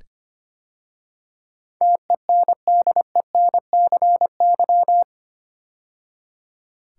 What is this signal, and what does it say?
Morse code, heard as a keyed tone, about 25 wpm, 700 Hz. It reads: TENDENCY
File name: 8 Letter Words - First 3 Letters Given - 25wpm